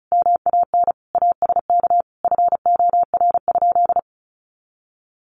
Morse code, heard as a keyed tone, about 34 wpm, 700 Hz. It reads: MAN ASK FOR?
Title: Sentences from Top 200 Words 35wpm